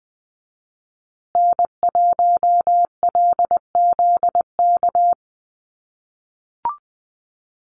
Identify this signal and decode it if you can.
Morse code, heard as a keyed tone, about 20 wpm, 700 Hz. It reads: N1LZK
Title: USA Call Signs Repeated 2x 20wpm